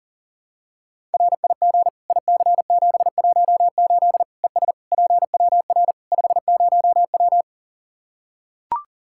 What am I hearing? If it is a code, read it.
Morse code, heard as a keyed tone, about 40 wpm, 700 Hz. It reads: RIG IC718 ES PWR 50W
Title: Practice QSO - Exchange 2 - Info - 40wpm